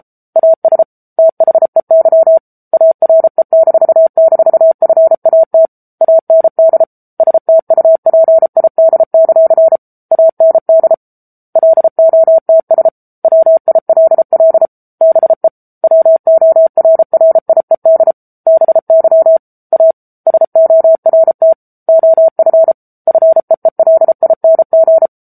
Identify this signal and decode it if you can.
Morse code, heard as a keyed tone, about 34 wpm, 650 Hz. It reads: AS THEY ARE--FAT AND STUPID; AND LOTS WILL BE WORRIED BY A SORT OF FEELING
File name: Book Two / 07-Chapter 7 - 33 WPM